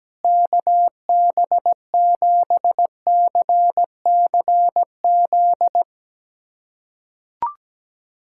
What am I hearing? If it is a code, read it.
Morse code, heard as a keyed tone, about 17 wpm, 700 Hz. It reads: KB7CCZ
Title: USA Callsign DE Callsign - 6 Characters - 17wpm